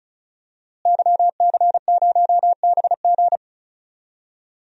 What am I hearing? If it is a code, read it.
Morse code, heard as a keyed tone, about 36 wpm, 700 Hz. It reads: YC0BG
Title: DXCC-100 Call Signs - 5 Characters - Repeated 2x - 35wpm